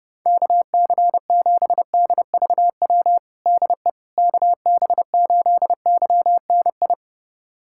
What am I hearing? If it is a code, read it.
Morse code, heard as a keyed tone, about 30 wpm, 700 Hz. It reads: KC7DVW DE KB8YNI
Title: USA Callsign DE Callsign 30wpm